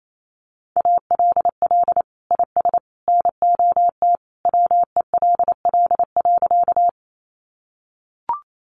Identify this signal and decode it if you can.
Morse code, heard as a keyed tone, about 28 wpm, 700 Hz. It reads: ALL IS NOT WELL.
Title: Sentences from Top 600 Words 28wpm